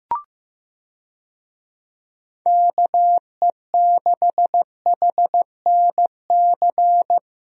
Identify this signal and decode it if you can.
Morse code, heard as a keyed tone, about 15 wpm, 700 Hz. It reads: KE6HNC
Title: USA Call Signs - 6 Characters - Repeated 2x - 15wpm